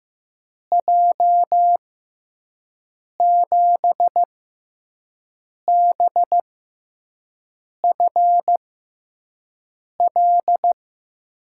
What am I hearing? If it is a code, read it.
Morse code, heard as a keyed tone, about 15 wpm, spaced as for 5 wpm, 700 Hz. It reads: J7BFL
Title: USA Call Signs - 6 Characters - 6x Character Spacing - 15wpm